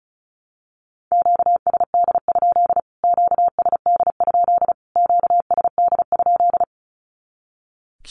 Morse code, heard as a keyed tone, about 34 wpm, 700 Hz. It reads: QSD? QSD? QSD?
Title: Q-Codes Repeated 3x 35wpm